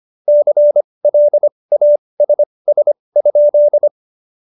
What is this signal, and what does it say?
Morse code, heard as a keyed tone, about 25 wpm, spaced as for 20 wpm, 590 Hz. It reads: CLASS?
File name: Jokes-25-char-20-eff